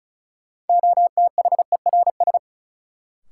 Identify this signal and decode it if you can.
Morse code, heard as a keyed tone, about 36 wpm, 700 Hz. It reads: OTHERS
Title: Sentences from Top 1500 Words - New Word Drill Alt - 35wpm